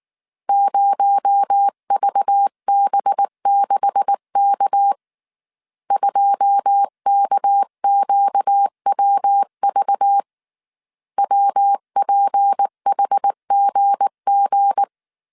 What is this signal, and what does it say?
Morse code, heard as a keyed tone, about 19 wpm, 800 Hz. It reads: 0VB6K 2KQWV WPHGG